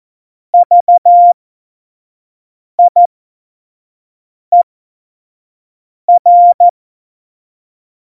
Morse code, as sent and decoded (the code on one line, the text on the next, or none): ...- .. .-.
VIER